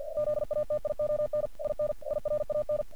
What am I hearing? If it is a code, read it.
Morse code, heard as a keyed tone, about 46 wpm, 600 Hz. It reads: ZATION IN IRAN
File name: archived / CRSnet